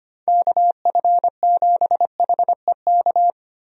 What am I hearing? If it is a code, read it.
Morse code, heard as a keyed tone, about 25 wpm, 700 Hz. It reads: KF7HEK
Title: USA Call Signs - 6 Characters - Repeated 3x - 25wpm